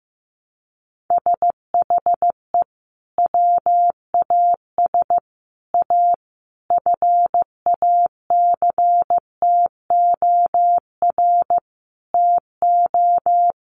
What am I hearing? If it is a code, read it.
Morse code, heard as a keyed tone, about 15 wpm, 700 Hz. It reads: SHE WAS A FACTOR TO
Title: Sentences from Top 1300 Words 15wpm